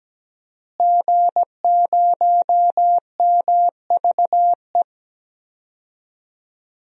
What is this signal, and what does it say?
Morse code, heard as a keyed tone, about 17 wpm, 700 Hz. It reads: G0MVE
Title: DXCC-100 Call Signs - 5 Characters - Repeated 3x - 17wpm